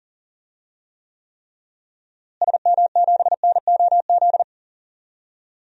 Morse code, heard as a keyed tone, about 40 wpm, 700 Hz. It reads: SM7NOZ
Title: DXCC-100 Call Signs - 6 Characters - Repeated 3x - 40wpm